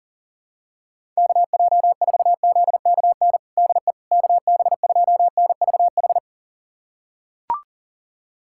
Morse code, heard as a keyed tone, about 40 wpm, 700 Hz. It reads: KJ4ZKN DE KB2NVH
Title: USA Callsign DE Callsign - 6 Characters - 40wpm